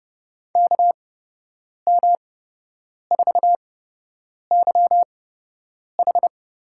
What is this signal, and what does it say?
Morse code, heard as a keyed tone, about 30 wpm, spaced as for 8 wpm, 700 Hz. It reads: KM4YH
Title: USA Call Signs - 8x Character Spacing - 30wpm